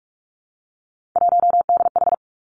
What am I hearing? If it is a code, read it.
Morse code, heard as a keyed tone, about 46 wpm, 700 Hz. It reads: 1DH
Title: Three Challenging Letter-Number Combinations 45wpm